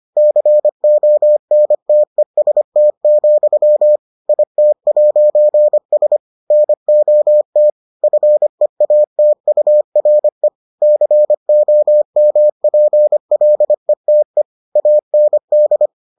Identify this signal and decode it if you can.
Morse code, heard as a keyed tone, about 25 wpm, 600 Hz. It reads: CONTEST, IT'S NOT FEATURE COMPLETE AND